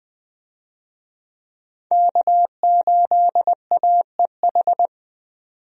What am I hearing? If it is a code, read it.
Morse code, heard as a keyed tone, about 20 wpm, 700 Hz. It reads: K8AEH
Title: USA Call Signs - 5 Characters - 20wpm